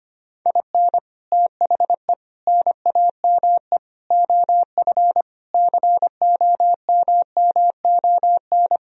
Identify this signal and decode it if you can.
Morse code, heard as a keyed tone, about 25 wpm, 700 Hz. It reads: IN THE NAME OF COMMON
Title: Sentences from Top 1000 Words - Review Part 4 - 25wpm